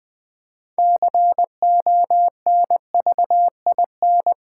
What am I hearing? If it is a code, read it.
Morse code, heard as a keyed tone, about 20 wpm, 700 Hz. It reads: CONVIN
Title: Build A Word - 3 Letters At A Time - 9-Letter Words - 20wpm